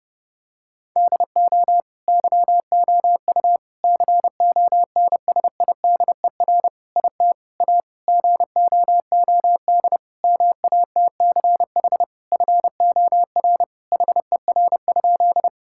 Morse code, heard as a keyed tone, about 30 wpm, 700 Hz. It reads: DO YOU CONSIDER IT A GOOD MATCH FOR HER?
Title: Sentences from Top 1600 Words 30wpm